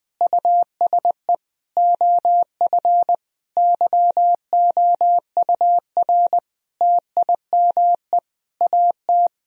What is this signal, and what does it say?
Morse code, heard as a keyed tone, about 20 wpm, 700 Hz. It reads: USE OF YOUR TIME AT
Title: Sentences from Top 1000 Words - Review Part 1 - 20wpm